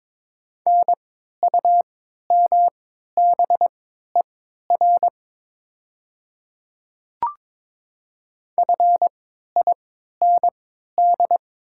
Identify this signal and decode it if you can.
Morse code, heard as a keyed tone, about 22 wpm, spaced as for 12 wpm, 700 Hz. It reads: NUMBER FIND
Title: 3x Character Spacing - Top 100 Words - 22wpm